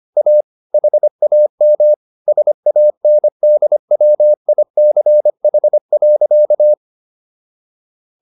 Morse code, heard as a keyed tone, about 25 wpm, 600 Hz. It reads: A HAM SANDWICH.